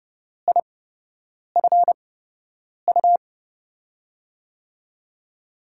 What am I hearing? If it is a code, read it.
Morse code, heard as a keyed tone, about 31 wpm, spaced as for 8 wpm, 700 Hz. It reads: IFU
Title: USA Call Signs - 6 Characters - 8x Character Spacing - 30wpm